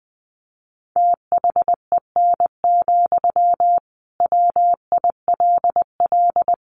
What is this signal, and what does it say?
Morse code, heard as a keyed tone, about 20 wpm, 700 Hz. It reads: THEN, WILL
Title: Sentences from Top 1400 Words 20wpm